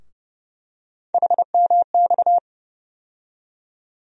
Morse code, heard as a keyed tone, about 30 wpm, 700 Hz. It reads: HMX
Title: USA Call Sign Postfixes - 3 Letters 30wpm